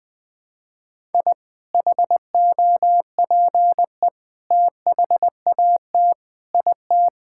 Morse code, heard as a keyed tone, about 21 wpm, 700 Hz. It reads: I HOPE THAT IT
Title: Sentences from Top 1000 Words - Review Part 5 - 20wpm